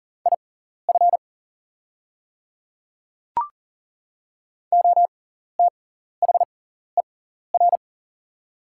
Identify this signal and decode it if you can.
Morse code, heard as a keyed tone, about 40 wpm, spaced as for 14 wpm, 700 Hz. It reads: IF OTHER
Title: 6x Character Spacing - Top 100 Words - 40wpm